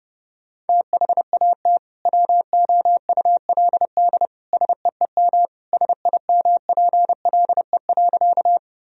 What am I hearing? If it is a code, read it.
Morse code, heard as a keyed tone, about 31 wpm, 700 Hz. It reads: THAT WOULD SEEM SIMPLE.
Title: Sentences from Top 1000 Words - Review Part 1 - 30wpm